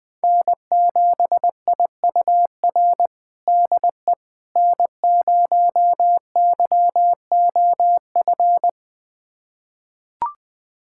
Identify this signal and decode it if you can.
Morse code, heard as a keyed tone, about 20 wpm, 700 Hz. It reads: N7IUR DE N0YOF
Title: USA Callsign DE Callsign - 5 Characters - 20wpm